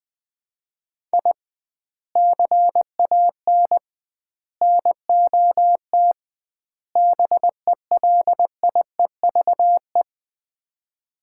Morse code, heard as a keyed tone, about 20 wpm, 700 Hz. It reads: I CAN NOT BELIEVE
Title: 2x Word-Spacing - Sets of 4 Words - 20wpm